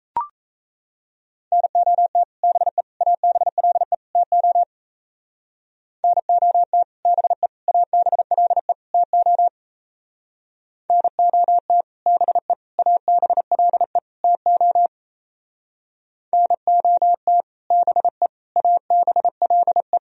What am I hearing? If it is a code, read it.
Morse code, heard as a keyed tone, about 34 wpm, 700 Hz. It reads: NOT BE ABLE TO NOT BE ABLE TO NOT BE ABLE TO NOT BE ABLE
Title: Speed-Racing - Sets of 4 Words 28wpm